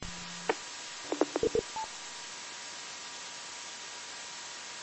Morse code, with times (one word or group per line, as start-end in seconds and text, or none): none